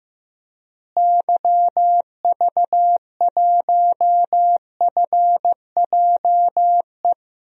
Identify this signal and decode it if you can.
Morse code, heard as a keyed tone, about 15 wpm, 700 Hz. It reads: YV1FJE
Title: DXCC-100 Call Signs - 6 Characters - Repeated 3x - 15wpm